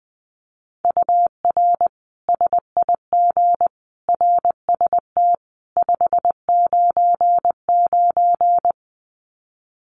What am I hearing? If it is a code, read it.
Morse code, heard as a keyed tone, about 20 wpm, 700 Hz. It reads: UR SIG RST 599
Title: Common QSO Phrases 20wpm